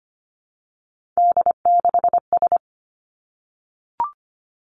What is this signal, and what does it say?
Morse code, heard as a keyed tone, about 25 wpm, 700 Hz. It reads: D6S
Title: Three Challenging Letter-Number Combinations 25wpm